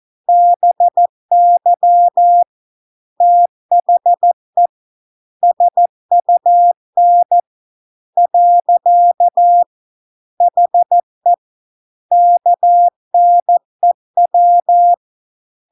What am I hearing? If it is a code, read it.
Morse code, heard as a keyed tone, about 14 wpm, 700 Hz. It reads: BY THE SUN . HE KNEW